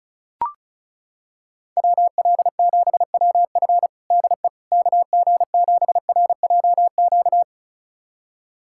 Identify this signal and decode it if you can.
Morse code, heard as a keyed tone, about 36 wpm, 690 Hz. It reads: WL7WF DE KG7RJQ
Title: USA Callsign DE Callsign 35wpm